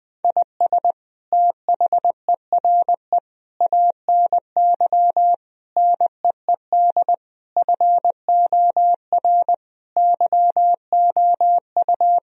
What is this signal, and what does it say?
Morse code, heard as a keyed tone, about 20 wpm, 700 Hz. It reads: IS THERE ANY NEED FOR YOU
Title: Sentences from Top 1000 Words - Review Part 4 - 20wpm